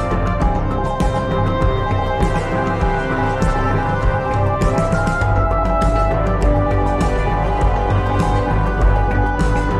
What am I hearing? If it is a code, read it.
Morse code, heard as a keyed tone, about 20 wpm, 820 Hz. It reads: THE FUTURE, FOR ME